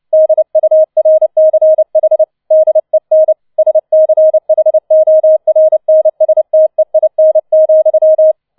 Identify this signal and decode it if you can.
Morse code, heard as a keyed tone, about 29 wpm, 630 Hz. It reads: DURCH DEN SCHORNSTEIN,
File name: 006 Sechster Streich 30wpm